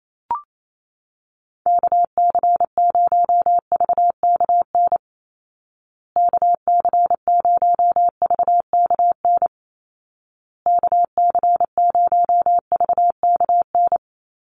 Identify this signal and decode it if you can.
Morse code, heard as a keyed tone, about 28 wpm, 700 Hz. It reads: KC0VKN KC0VKN KC0VKN
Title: Call Signs Repeated 2x - 28wpm